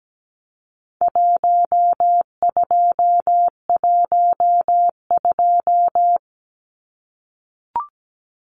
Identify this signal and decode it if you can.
Morse code, heard as a keyed tone, about 17 wpm, 700 Hz. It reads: 1212
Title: Numbers - Contest Style - 4-Digit 17wpm